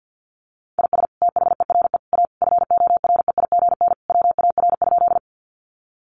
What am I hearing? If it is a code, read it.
Morse code, heard as a keyed tone, about 50 wpm, 700 Hz. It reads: IS THERE A FOREIGN WAR?
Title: Sentences from Top 1400 Words 50wpm